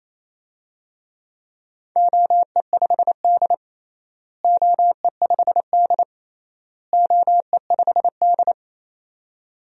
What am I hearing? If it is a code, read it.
Morse code, heard as a keyed tone, about 28 wpm, 700 Hz. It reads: OE5D OE5D OE5D
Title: DXCC-100 Call Signs - 4 Characters - Repeated 3x - 28wpm